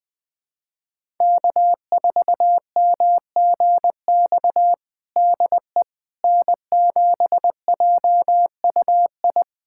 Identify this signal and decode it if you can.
Morse code, heard as a keyed tone, about 20 wpm, 700 Hz. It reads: K4MGX DE N7JUI
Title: USA Callsign DE Callsign - 5 Characters - 20wpm